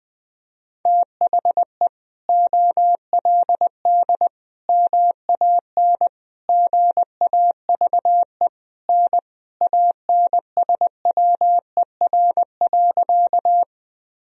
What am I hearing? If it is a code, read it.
Morse code, heard as a keyed tone, about 20 wpm, 700 Hz. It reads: THE OLD MAN GAVE N ANSWER.
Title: Sentences from Top 500 Words and Abbreviations 20wpm